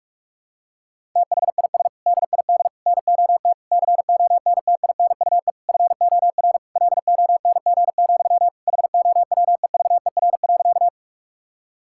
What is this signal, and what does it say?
Morse code, heard as a keyed tone, about 46 wpm, 700 Hz. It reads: THIS DID NOT CONTINUE FOR LONG, HOWEVER.
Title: Sentences from Top 1000 Words - Review Part 6 - 45wpm